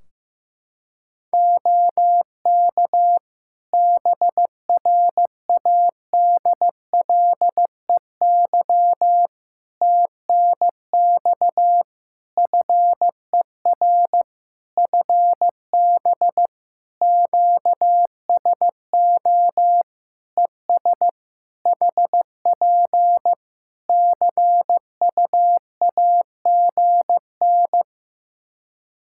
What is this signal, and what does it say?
Morse code, heard as a keyed tone, about 15 wpm, 700 Hz. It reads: OK BRADLEY TNX FER FB QSO ES HP CUAGN
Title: Practice QSO - Exchange 4 - Ending - 15wpm